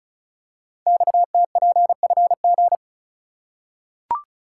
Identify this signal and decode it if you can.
Morse code, heard as a keyed tone, about 36 wpm, 700 Hz. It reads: XTPFG
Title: Code Groups - 5 Letters - 35wpm